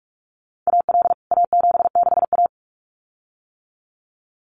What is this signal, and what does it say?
Morse code, heard as a keyed tone, about 44 wpm, 700 Hz. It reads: AL A76A
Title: CWT Exchange 45wpm